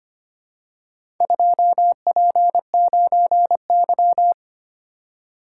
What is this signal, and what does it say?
Morse code, heard as a keyed tone, about 25 wpm, 700 Hz. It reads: 2P9Y
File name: practice / 25wpm / Four Letter-Number Combinations - 25wpm